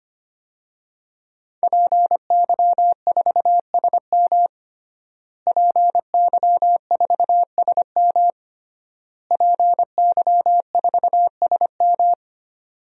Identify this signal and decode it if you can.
Morse code, heard as a keyed tone, about 25 wpm, 700 Hz. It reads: PY4SM PY4SM PY4SM
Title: DXCC-100 Call Signs - 5 Characters - Repeated 3x - 25wpm